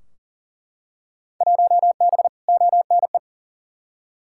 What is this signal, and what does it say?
Morse code, heard as a keyed tone, about 40 wpm, 700 Hz. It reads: 1B ONE
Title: Field Day Exchanges - 40wpm